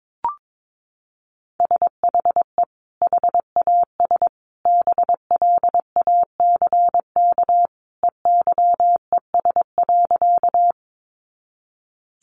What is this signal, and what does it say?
Morse code, heard as a keyed tone, about 22 wpm, 700 Hz. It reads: SHE HAS BLACK EYES.
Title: Easy Phrases 22wpm